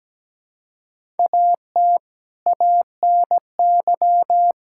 Morse code, heard as a keyed tone, about 17 wpm, 700 Hz. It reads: AT ANY
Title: Sentences from Top 1000 Words - Review Part 5 - 17wpm